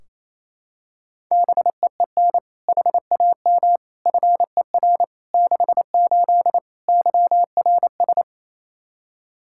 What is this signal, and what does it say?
Morse code, heard as a keyed tone, about 28 wpm, 700 Hz. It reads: BEEN HAM FER 68 YRS
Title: K1USN SST QSO Practice 28wpm